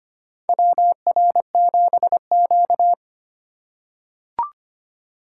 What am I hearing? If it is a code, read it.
Morse code, heard as a keyed tone, about 25 wpm, 700 Hz. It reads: WR7Q
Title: USA Call Signs - 4x Character Spacing - 25wpm